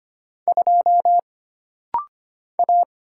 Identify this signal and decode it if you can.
Morse code, heard as a keyed tone, about 25 wpm, 700 Hz. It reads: 2 A
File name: Single Character (Letter:Number:Punctuation:Prosign) - Rapid-Fire - 25wpm